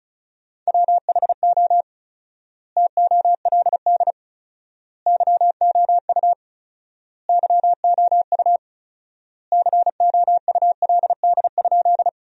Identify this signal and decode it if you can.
Morse code, heard as a keyed tone, about 36 wpm, 700 Hz. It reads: WHO TOLD YOU YOU COULD?
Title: 4x Word-Spacing - Sentences from Top 400 Words - 35wpm